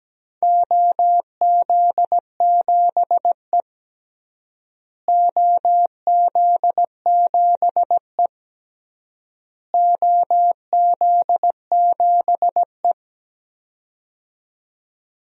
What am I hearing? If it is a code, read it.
Morse code, heard as a keyed tone, about 17 wpm, 700 Hz. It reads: OZ7E OZ7E OZ7E
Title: DXCC-100 Call Signs - 4 Characters - Repeated 3x - 17wpm